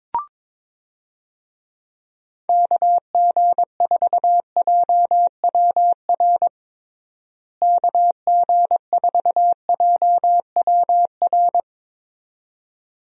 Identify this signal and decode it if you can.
Morse code, heard as a keyed tone, about 22 wpm, 700 Hz. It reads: KG4JWR KG4JWR
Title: USA Call Signs - 6 Characters - Repeated 2x - 22wpm